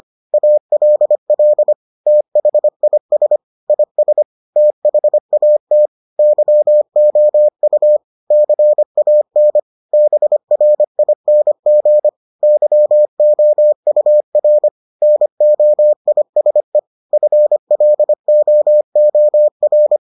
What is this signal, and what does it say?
Morse code, heard as a keyed tone, about 25 wpm, 600 Hz. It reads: ALL THIS IS THAT YOU CAN BRING YOUR NOISE FLOOR